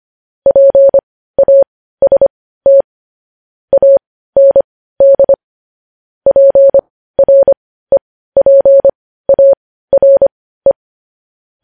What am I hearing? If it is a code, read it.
Morse code, heard as a keyed tone, about 25 wpm, spaced as for 15 wpm, 550 Hz. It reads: PAST AND PREPARE
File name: podcast / News headlines-1x-25x15